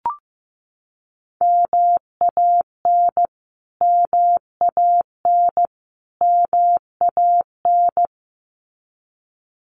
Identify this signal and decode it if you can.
Morse code, heard as a keyed tone, about 15 wpm, 700 Hz. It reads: MAN MAN MAN
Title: Top 300 Words Repeated 15wpm